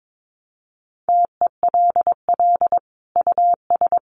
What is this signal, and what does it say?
Morse code, heard as a keyed tone, about 22 wpm, 700 Hz. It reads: TELL US
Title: Sentences from Top 1200 Words 22wpm